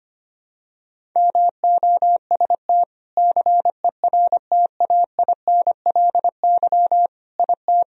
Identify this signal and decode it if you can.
Morse code, heard as a keyed tone, about 25 wpm, 700 Hz. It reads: MOST CERTAINLY IT
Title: Sentences from Top 1000 Words - Review Part 1 - 25wpm